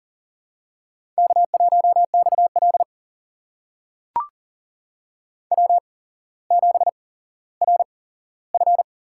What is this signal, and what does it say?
Morse code, heard as a keyed tone, about 38 wpm, spaced as for 17 wpm, 700 Hz. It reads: K1XL W7RF
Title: USA Call Signs - 4 Characters - 8x Character Spacing - 40wpm